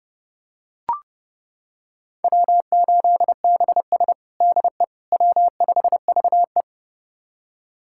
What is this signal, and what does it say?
Morse code, heard as a keyed tone, about 30 wpm, 700 Hz. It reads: W8BS DE W5VE
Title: USA Callsign DE Callsign - 4 Characters - 30wpm